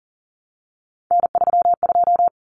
Morse code, heard as a keyed tone, about 40 wpm, 700 Hz. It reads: N32
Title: Common Cut Numbers - 3-Digit 40wpm